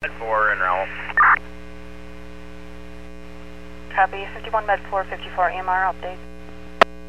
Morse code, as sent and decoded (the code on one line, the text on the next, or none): none